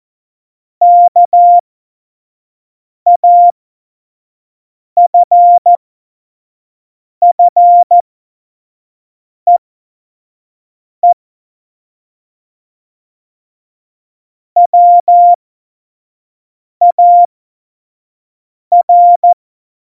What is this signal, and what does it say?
Morse code, heard as a keyed tone, about 14 wpm, spaced as for 5 wpm, 700 Hz. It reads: KAFFEE WAR
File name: Morsekurs / Witze / Witz 13. August 2019, 25 F70